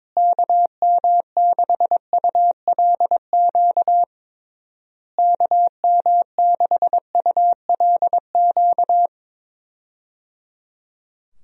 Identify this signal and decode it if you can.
Morse code, heard as a keyed tone, about 21 wpm, 700 Hz. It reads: KM6ULQ KM6ULQ
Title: USA Call Signs - 6 Characters - Repeated 3x - 22wpm